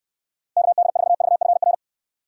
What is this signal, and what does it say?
Morse code, heard as a keyed tone, about 69 wpm, 700 Hz. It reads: KD4VFR